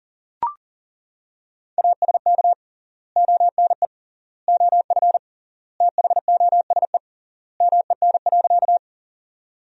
Morse code, heard as a keyed tone, about 40 wpm, 700 Hz. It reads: ASK ONE OF THOSE MEN.
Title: 3x Word-Spacing - Sentences from Top 300 Words - 40wpm